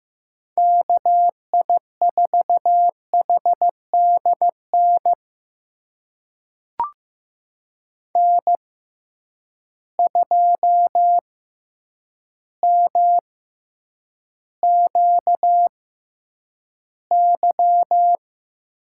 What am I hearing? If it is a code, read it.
Morse code, heard as a keyed tone, about 15 wpm, spaced as for 8 wpm, 700 Hz. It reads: KI4HDN N2MQY
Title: USA Call Signs - 6x Character Spacing - 15wpm